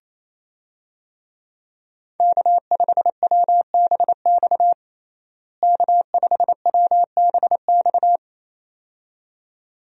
Morse code, heard as a keyed tone, about 28 wpm, 700 Hz. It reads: K5WBX K5WBX
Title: USA Call Signs - 5 Characters - Repeated 2x - 28wpm